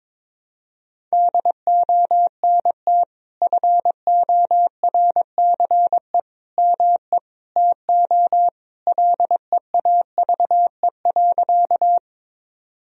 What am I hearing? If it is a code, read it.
Morse code, heard as a keyed tone, about 22 wpm, 700 Hz. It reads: DONT FORCE ME TO LEAVE.
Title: Sentences from Top 1000 Words - Review Part 3 - 22wpm